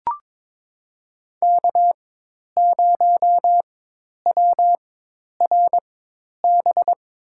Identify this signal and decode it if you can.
Morse code, heard as a keyed tone, about 22 wpm, spaced as for 10 wpm, 700 Hz. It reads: K0WRB
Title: USA Call Signs - 5 Characters - 4x Character Spacing - 22wpm